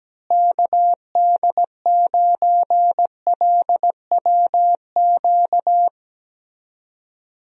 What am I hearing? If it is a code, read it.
Morse code, heard as a keyed tone, about 17 wpm, 700 Hz. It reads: KD9LWQ
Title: USA Call Signs - 6 Characters - Repeated 3x - 17wpm